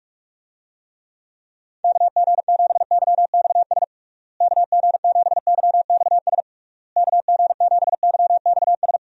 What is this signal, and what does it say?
Morse code, heard as a keyed tone, about 47 wpm, 700 Hz. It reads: KG7YXS KG7YXS KG7YXS
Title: USA Call Signs - 6 Characters - Repeated 3x - 45wpm